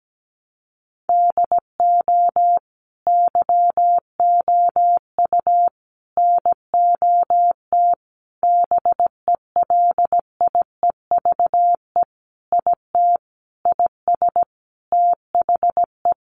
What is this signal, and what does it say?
Morse code, heard as a keyed tone, about 17 wpm, 700 Hz. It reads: DO YOU NOT BELIEVE IT IS THE
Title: Sentences from Top 1000 Words 17wpm